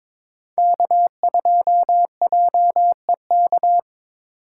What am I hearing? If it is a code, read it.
Morse code, heard as a keyed tone, about 22 wpm, 700 Hz. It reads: K2JEK
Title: USA Call Signs - 3x Character Spacing - 22wpm